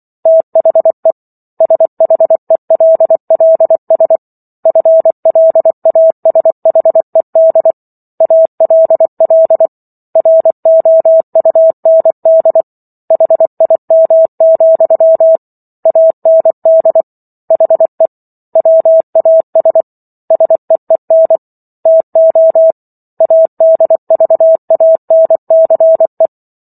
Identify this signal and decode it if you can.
Morse code, heard as a keyed tone, about 24 wpm, 650 Hz. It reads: THE SHELLS FLASHED ALL ROUND HIM, AND HE WAS SEEN TO ADVANCE